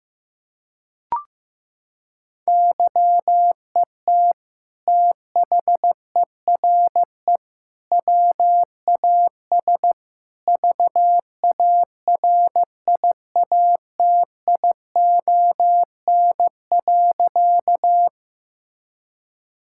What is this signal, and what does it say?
Morse code, heard as a keyed tone, about 15 wpm, 700 Hz. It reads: YET THERE WAS VARIATION.
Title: Sentences from Top 1600 Words 15wpm